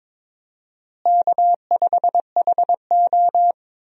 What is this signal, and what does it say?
Morse code, heard as a keyed tone, about 22 wpm, 700 Hz. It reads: K5HO
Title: USA Call Signs - 4 Characters - Repeated 3x - 22wpm